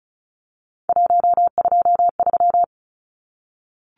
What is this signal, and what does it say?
Morse code, heard as a keyed tone, about 34 wpm, 700 Hz. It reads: 123
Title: Numbers - 3-Digit 35wpm